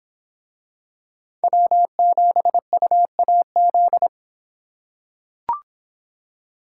Morse code, heard as a keyed tone, about 24 wpm, 700 Hz. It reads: W7UAZ